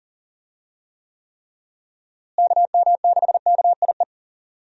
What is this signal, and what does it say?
Morse code, heard as a keyed tone, about 40 wpm, 700 Hz. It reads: KM6KIE
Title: USA Call Signs - 6 Characters - Repeated 3x - 40wpm